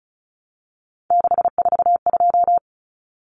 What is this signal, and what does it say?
Morse code, heard as a keyed tone, about 34 wpm, 700 Hz. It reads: 642
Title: Numbers - 3-Digit 35wpm